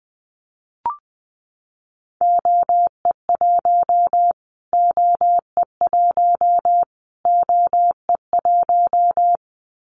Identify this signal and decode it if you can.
Morse code, heard as a keyed tone, about 20 wpm, 700 Hz. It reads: OE1 OE1 OE1
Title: Callsign Prefixes for Top 25 DXCC - Repeated 3x - 20wpm